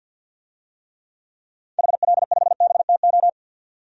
Morse code, heard as a keyed tone, about 48 wpm, 700 Hz. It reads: HL5BTO
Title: DXCC-100 Call Signs - 6 Characters - 50wpm